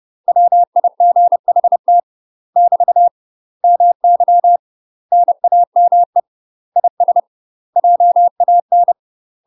Morse code, heard as a keyed tone, about 30 wpm, 700 Hz. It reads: WIGHT <BT> MY NAME IS JAN
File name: UK MASTER QSOs TRAINING / 30WPM / UK MASTER QSOs 0024